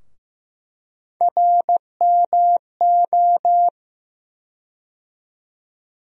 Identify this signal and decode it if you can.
Morse code, heard as a keyed tone, about 15 wpm, 700 Hz. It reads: RMO
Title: USA Call Sign Postfixes - 3 Letters 15wpm